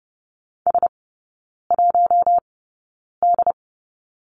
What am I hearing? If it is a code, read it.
Morse code, heard as a keyed tone, about 30 wpm, spaced as for 9 wpm, 700 Hz. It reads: S1D
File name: Lesson 19e - Letter w - Three Characters 30wpm